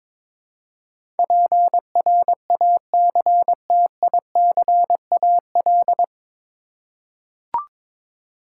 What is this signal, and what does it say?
Morse code, heard as a keyed tone, about 22 wpm, 700 Hz. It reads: PRACTICAL
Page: Morse Code Ninja practice 22wpm